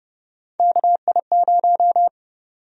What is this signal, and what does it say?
Morse code, heard as a keyed tone, about 30 wpm, 700 Hz. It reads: KI0
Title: USA Call Sign Prefixes - 30wpm